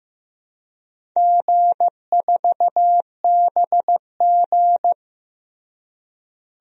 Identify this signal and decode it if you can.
Morse code, heard as a keyed tone, about 15 wpm, 700 Hz. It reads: G4BG
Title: DXCC-100 Call Signs - 4 Characters - Repeated 2x - 15wpm